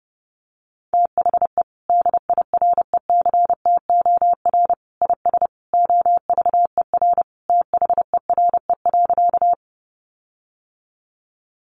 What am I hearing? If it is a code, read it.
Morse code, heard as a keyed tone, about 30 wpm, 700 Hz. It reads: THE DIRECTOR IS OVER THERE.
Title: Sentences from Top 1300 Words 30wpm